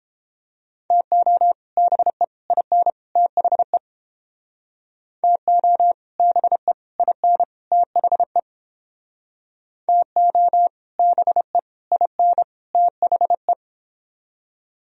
Morse code, heard as a keyed tone, about 30 wpm, 700 Hz. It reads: TO BE IN THE TO BE IN THE TO BE IN THE